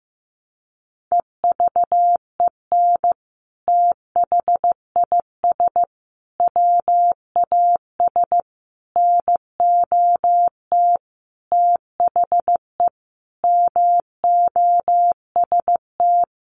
Morse code, heard as a keyed tone, about 15 wpm, 700 Hz. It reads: EVEN THIS WAS NOT THE MOST
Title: Sentences from Top 1100 Words 15wpm